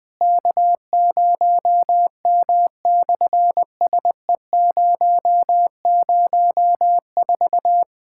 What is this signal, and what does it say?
Morse code, heard as a keyed tone, about 20 wpm, 700 Hz. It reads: K0M/SE004
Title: SOTA QSO Practice 20wpm